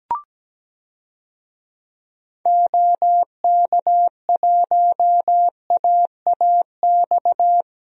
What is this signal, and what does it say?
Morse code, heard as a keyed tone, about 17 wpm, 700 Hz. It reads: OK1AAX
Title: DXCC-100 Call Signs - 6 Characters - Repeated 2x - 17wpm